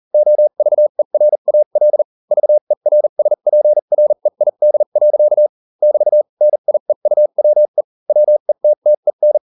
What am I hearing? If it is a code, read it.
Morse code, heard as a keyed tone, about 40 wpm, spaced as for 35 wpm, 600 Hz. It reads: OVERAL VERSPREID. <BT> NIEUWE WETTEN